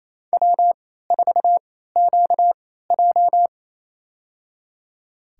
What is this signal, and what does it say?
Morse code, heard as a keyed tone, about 27 wpm, spaced as for 16 wpm, 700 Hz. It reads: W4QJ
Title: USA Call Signs - 3x Character Spacing - 28wpm